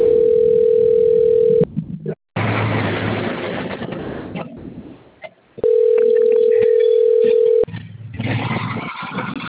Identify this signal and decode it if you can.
Ringback tone (North America), 440 + 480 Hz.